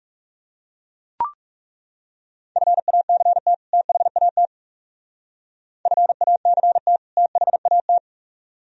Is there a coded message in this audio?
Yes: Morse code from 2.6 to 8.0 s, heard as a keyed tone, about 38 wpm, 700 Hz. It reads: FACT THAT FACT THAT